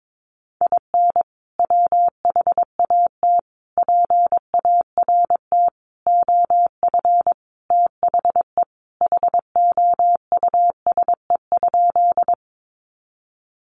Morse code, heard as a keyed tone, about 22 wpm, 700 Hz. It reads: IN WHAT PART OF THE HOUSE?
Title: Sentences from Top 200 Words 22wpm